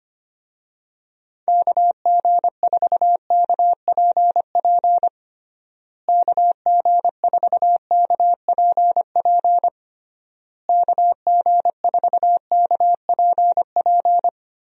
Morse code, heard as a keyed tone, about 25 wpm, 700 Hz. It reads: KG4KPP KG4KPP KG4KPP
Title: USA Call Signs - 6 Characters - Repeated 3x - 25wpm